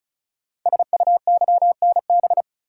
Morse code, heard as a keyed tone, about 36 wpm, 690 Hz. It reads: SUYNB
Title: Code Groups - 5 Letters - 35wpm